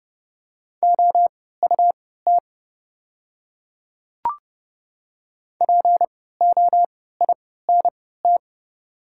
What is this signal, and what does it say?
Morse code, heard as a keyed tone, about 30 wpm, spaced as for 17 wpm, 690 Hz. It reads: OUT POINT
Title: 3x Character Spacing - Top 200 Words - 30wpm